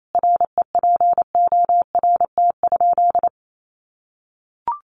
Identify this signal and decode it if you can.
Morse code, heard as a keyed tone, about 28 wpm, 700 Hz. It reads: REPORT?
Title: Sentences from Top 1300 Words 28wpm